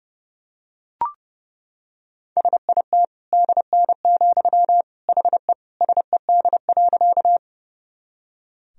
Morse code, heard as a keyed tone, about 30 wpm, 700 Hz. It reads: SIT DN, HE SED.
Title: Sentences from Top 500 Words and Abbreviations 30wpm